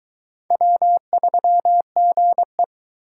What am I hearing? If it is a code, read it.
Morse code, heard as a keyed tone, about 20 wpm, 700 Hz. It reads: W3GE